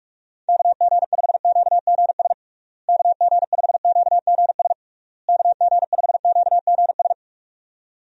Morse code, heard as a keyed tone, about 46 wpm, 700 Hz. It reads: KG5QGS KG5QGS KG5QGS
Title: USA Call Signs - 6 Characters - Repeated 3x - 45wpm